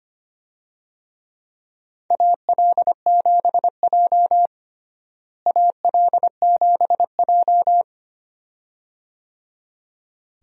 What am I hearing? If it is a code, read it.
Morse code, heard as a keyed tone, about 25 wpm, 700 Hz. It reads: AL7J AL7J
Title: USA Call Signs - 4 Characters - Repeated 2x - 25wpm